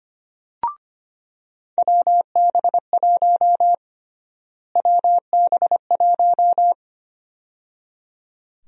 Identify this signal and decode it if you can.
Morse code, heard as a keyed tone, about 25 wpm, 700 Hz. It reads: WB1 WB1
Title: USA Call Sign Prefixes - Repeated 2x - 25wpm